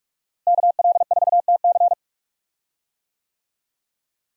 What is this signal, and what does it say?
Morse code, heard as a keyed tone, about 46 wpm, 700 Hz. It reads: KL4TC
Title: USA Call Signs - 5 Characters - Repeated 3x - 45wpm